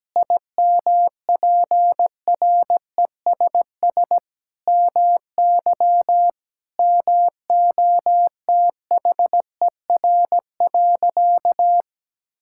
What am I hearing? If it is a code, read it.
Morse code, heard as a keyed tone, about 17 wpm, 700 Hz. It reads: IMPRESS MY MOTHER.
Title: Sentences from Top 1600 Words 17wpm